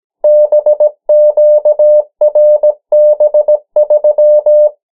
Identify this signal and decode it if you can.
Morse code, heard as a keyed tone, about 17 wpm, 600 Hz. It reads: BQRB3
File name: morse mp3 / 14 urok 13